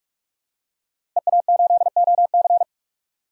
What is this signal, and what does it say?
Morse code, heard as a keyed tone, about 46 wpm, 700 Hz. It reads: EA8OC
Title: DXCC-100 Call Signs - 5 Characters - Repeated 3x - 45wpm